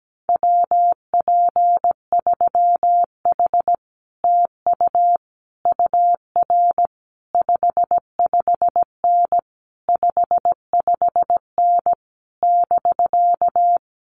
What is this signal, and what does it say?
Morse code, heard as a keyed tone, about 17 wpm, 700 Hz. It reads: WP3H TU UR 55N 55N <BK>
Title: Common QSO Phrases Encore 17wpm